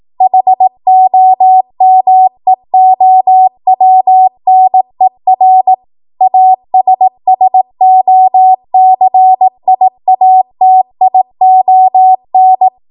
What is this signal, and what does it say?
Morse code, heard as a keyed tone, about 18 wpm, 750 Hz. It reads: HOMEOWNER ASSOCIATION
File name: Morse / 18WPM